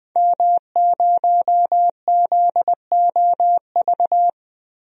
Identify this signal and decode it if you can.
Morse code, heard as a keyed tone, about 20 wpm, 700 Hz. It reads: M0ZOV
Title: DXCC-100 Call Signs - 5 Characters - 20wpm